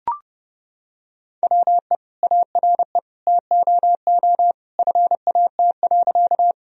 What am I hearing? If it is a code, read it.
Morse code, heard as a keyed tone, about 30 wpm, 700 Hz. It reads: WE ARE TOO FAT.
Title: Sentences from Top 1000 Words - Review Part 4 - 30wpm